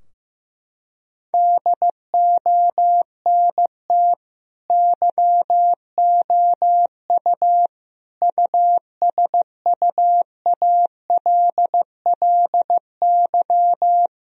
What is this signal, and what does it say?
Morse code, heard as a keyed tone, about 15 wpm, 700 Hz. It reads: DONT YOU USUALLY
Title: Sentences from Top 1000 Words - Review Part 3 - 15wpm